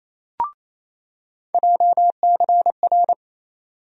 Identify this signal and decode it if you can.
Morse code, heard as a keyed tone, about 28 wpm, 700 Hz. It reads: JCR
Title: USA Call Sign Postfixes - 3 Letters 28wpm